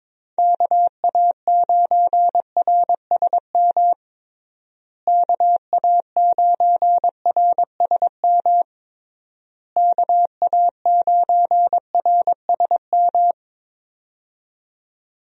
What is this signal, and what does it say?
Morse code, heard as a keyed tone, about 22 wpm, 700 Hz. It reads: KA9RSM KA9RSM KA9RSM
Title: USA Call Signs - 6 Characters - Repeated 3x - 22wpm